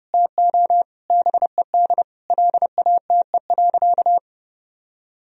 Morse code, heard as a keyed tone, about 29 wpm, 700 Hz. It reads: TO BED LATE.
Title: Sentences from Top 1000 Words - Review Part 4 - 30wpm